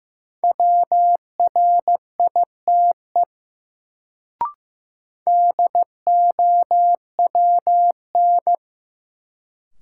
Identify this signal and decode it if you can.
Morse code, heard as a keyed tone, about 15 wpm, 700 Hz. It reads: WRITE DOWN
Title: Sentences from Top 300 Words - New Word Drill - 15wpm